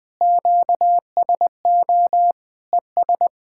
Morse code, heard as a keyed tone, about 19 wpm, 700 Hz. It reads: QSO ES
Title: Speed-Racing Common QSO Phrases Encore - 20wpm